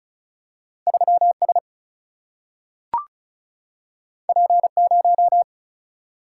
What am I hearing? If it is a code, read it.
Morse code, heard as a keyed tone, about 36 wpm, 700 Hz. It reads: 3S P0
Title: Single-Letter Single-Number 35wpm